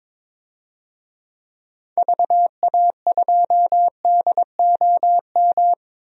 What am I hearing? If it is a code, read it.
Morse code, heard as a keyed tone, about 22 wpm, 700 Hz. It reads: VA2DOM
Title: DXCC-100 Call Signs - 6 Characters - Repeated 3x - 22wpm